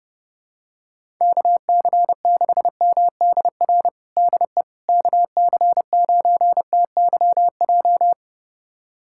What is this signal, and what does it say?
Morse code, heard as a keyed tone, about 30 wpm, 700 Hz. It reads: KC6MDR DE KC9TYJ
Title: USA Callsign DE Callsign - 6 Characters - 30wpm